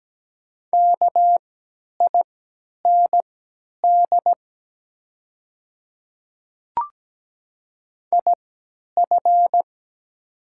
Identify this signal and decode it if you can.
Morse code, heard as a keyed tone, about 17 wpm, spaced as for 10 wpm, 700 Hz. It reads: KIND IF
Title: 3x Character Spacing - Top 300 Words - 17wpm